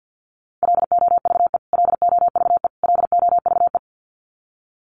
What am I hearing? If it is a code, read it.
Morse code, heard as a keyed tone, about 50 wpm, 700 Hz. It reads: LOVE LOVE LOVE
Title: Top 500 Words Repeated 50wpm